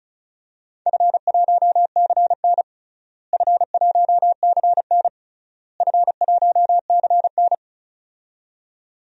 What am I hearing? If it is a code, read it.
Morse code, heard as a keyed tone, about 36 wpm, 700 Hz. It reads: F1CN F1CN F1CN
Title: DXCC-100 Call Signs - 4 Characters - Repeated 3x - 35wpm